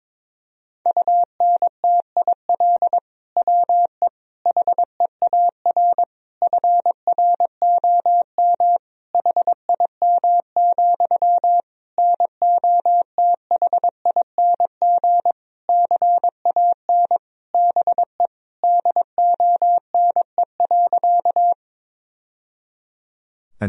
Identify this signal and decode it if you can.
Morse code, heard as a keyed tone, about 22 wpm, 700 Hz. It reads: UNTIL WE HEAR FROM HIM, NOTHING CAN BE DONE.
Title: Sentences from Top 1000 Words - Review Part 1 - 22wpm